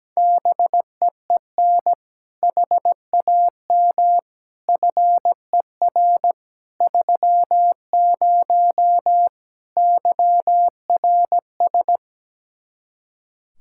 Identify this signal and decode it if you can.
Morse code, heard as a keyed tone, about 17 wpm, 700 Hz. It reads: BEEN HAM FER 30 YRS
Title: Speed-Racing Common QSO Phrases - 17wpm